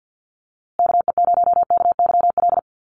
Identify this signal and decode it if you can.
Morse code, heard as a keyed tone, about 50 wpm, 700 Hz. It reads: KE0KYL
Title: USA Call Signs Repeated 3x 50wpm